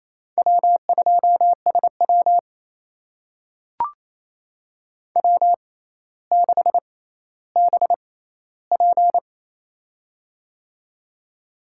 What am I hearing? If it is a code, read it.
Morse code, heard as a keyed tone, about 28 wpm, spaced as for 14 wpm, 700 Hz. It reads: W2SW W6BP
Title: USA Call Signs - 4 Characters - 6x Character Spacing - 28wpm